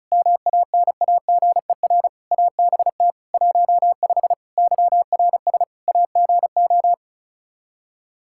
Morse code, heard as a keyed tone, about 36 wpm, 700 Hz. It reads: MANAGER ABT 15 YRS AGO
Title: Practice QSO - Exchange 3 - More Info - 35wpm